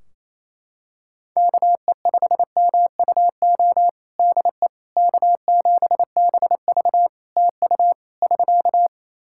Morse code, heard as a keyed tone, about 28 wpm, 700 Hz. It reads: KE5MUO DE K7BV TU <SK>
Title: K1USN SST QSO Practice 28wpm